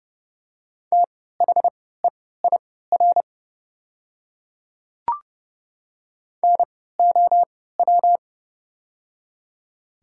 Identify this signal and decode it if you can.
Morse code, heard as a keyed tone, about 30 wpm, spaced as for 17 wpm, 700 Hz. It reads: THEIR NOW
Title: 3x Character Spacing - Top 100 Words - 30wpm